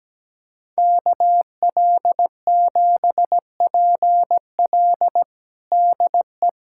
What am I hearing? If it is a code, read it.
Morse code, heard as a keyed tone, about 17 wpm, 700 Hz. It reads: KL7PL DE
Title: USA Callsign DE Callsign - 5 Characters - 17wpm